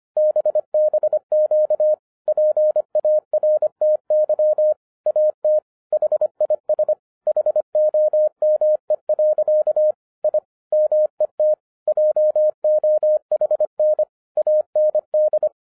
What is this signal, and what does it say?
Morse code, heard as a keyed tone, about 25 wpm, 610 Hz. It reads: BBQ PARTY AT HIS HOME. I MET JOHN AND